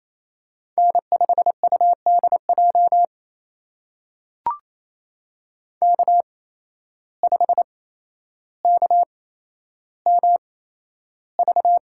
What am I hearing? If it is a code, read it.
Morse code, heard as a keyed tone, about 28 wpm, spaced as for 12 wpm, 700 Hz. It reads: N5UDJ K5KMV
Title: USA Call Signs - 5 Characters - 8x Character Spacing - 28wpm